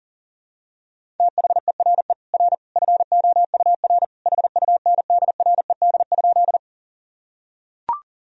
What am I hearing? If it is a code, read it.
Morse code, heard as a keyed tone, about 40 wpm, 700 Hz. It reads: THERE R FOUR HUNDRED?
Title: Sentences from Top 400 Words and Abbreviations 40wpm